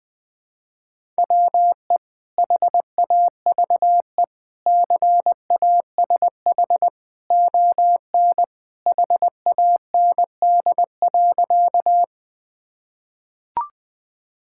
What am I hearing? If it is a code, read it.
Morse code, heard as a keyed tone, about 20 wpm, 700 Hz. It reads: WE HAVE CASH ON HAND.
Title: Sentences from Top 1000 Words - Review Part 5 - 20wpm